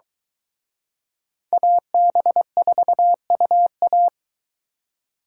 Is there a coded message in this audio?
Yes: Morse code from 1.5 to 4.1 s, heard as a keyed tone, about 22 wpm, 700 Hz. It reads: AB4UA